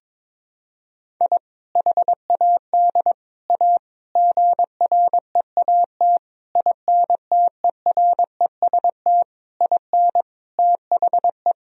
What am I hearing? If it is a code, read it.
Morse code, heard as a keyed tone, about 21 wpm, 700 Hz. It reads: I HAD A GREAT INTEREST IN THE
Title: Sentences from Top 1000 Words - Review Part 2 - 22wpm